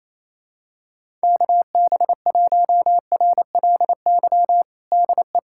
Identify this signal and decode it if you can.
Morse code, heard as a keyed tone, about 27 wpm, 700 Hz. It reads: KB1RLY DE
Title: USA Callsign DE Callsign 28wpm